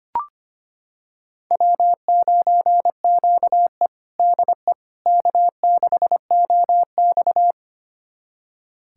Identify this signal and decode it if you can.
Morse code, heard as a keyed tone, about 25 wpm, 700 Hz. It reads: W9QE DE K6OX
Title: USA Callsign DE Callsign - 4 Characters - 25wpm